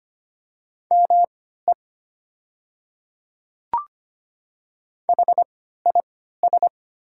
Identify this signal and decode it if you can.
Morse code, heard as a keyed tone, about 25 wpm, spaced as for 14 wpm, 700 Hz. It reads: ME HIS
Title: 3x Character Spacing - Top 300 Words - 25wpm